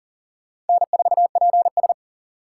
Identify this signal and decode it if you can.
Morse code, heard as a keyed tone, about 51 wpm, 700 Hz. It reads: N4PS